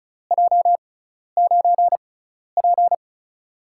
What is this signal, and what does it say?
Morse code, heard as a keyed tone, about 36 wpm, spaced as for 12 wpm, 700 Hz. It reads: J9P
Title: USA Call Signs - 4 Characters - 6x Character Spacing - 35wpm